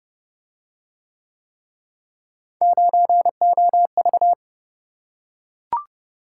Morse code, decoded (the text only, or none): 9OV